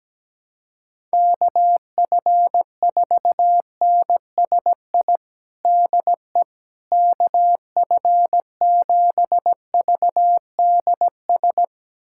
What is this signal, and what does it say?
Morse code, heard as a keyed tone, about 17 wpm, 700 Hz. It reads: KF4NSI DE KF7VDS
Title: USA Callsign DE Callsign - 6 Characters - 17wpm